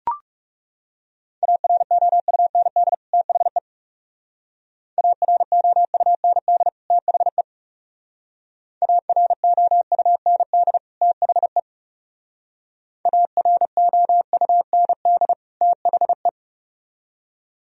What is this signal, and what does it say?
Morse code, heard as a keyed tone, about 37 wpm, 700 Hz. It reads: AROUND THE AROUND THE AROUND THE AROUND THE
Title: Speed-Racing - Sets of 2 Words 30wpm